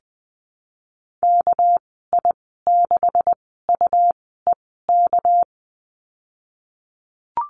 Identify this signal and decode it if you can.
Morse code, heard as a keyed tone, about 20 wpm, spaced as for 15 wpm, 700 Hz. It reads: KI6UEK
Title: 231 Lesson 37g - Letter x - Call Signs 20wpm